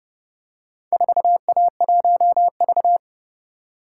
Morse code, heard as a keyed tone, about 28 wpm, 700 Hz. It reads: 4A1V